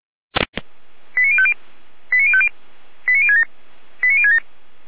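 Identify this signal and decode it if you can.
Five-tone selective call (ZVEI), heard as quick runs of five tones, tones of about 70 ms. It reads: 89050 89050 89067 89067
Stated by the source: ZVEI five-tone selective call 89010 received on SDR, from the Feuerwehr Orlamünde (fire brigade) audio server.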